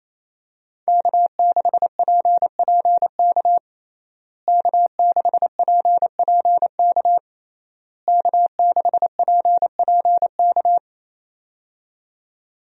Morse code, heard as a keyed tone, about 28 wpm, 700 Hz. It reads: K6PPK K6PPK K6PPK
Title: USA Call Signs - 5 Characters - Repeated 3x - 28wpm